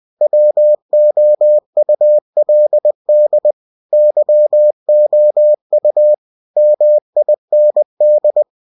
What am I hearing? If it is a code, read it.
Morse code, heard as a keyed tone, about 20 wpm, 600 Hz. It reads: WOULD YOU MIND
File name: cw-books / winnie-the-pooh / winnie-20wpm-page-0022